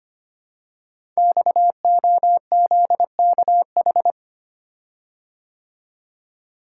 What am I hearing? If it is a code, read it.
Morse code, heard as a keyed tone, about 25 wpm, 700 Hz. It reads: XOZKH
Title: Code Groups - 5 Letter:Numbers - 25wpm